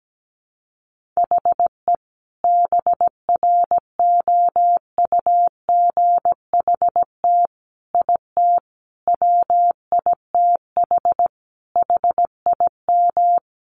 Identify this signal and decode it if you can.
Morse code, heard as a keyed tone, about 17 wpm, 700 Hz. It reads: HE BROUGHT IT WITH HIM
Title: Sentences from Top 500 Words 17wpm